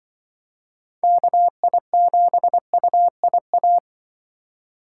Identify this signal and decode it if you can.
Morse code, heard as a keyed tone, about 25 wpm, 700 Hz. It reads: KI7UIA
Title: Speed Racing - USA Call Signs 20wpm